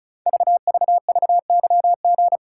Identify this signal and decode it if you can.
Morse code, heard as a keyed tone, about 36 wpm, 700 Hz. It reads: VVVYG
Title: Code Groups - 5 Letters - 35wpm